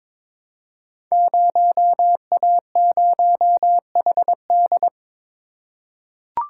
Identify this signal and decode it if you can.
Morse code, heard as a keyed tone, about 22 wpm, 700 Hz. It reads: 0A0HD
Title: Code Groups - 5 Letter:Numbers - 22wpm